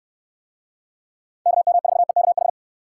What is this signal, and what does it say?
Morse code, heard as a keyed tone, about 69 wpm, 700 Hz. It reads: KM4EKH